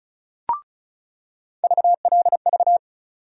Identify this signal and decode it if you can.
Morse code, heard as a keyed tone, about 36 wpm, 700 Hz. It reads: VLV
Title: USA Call Sign Postfixes - 3 Letters 35wpm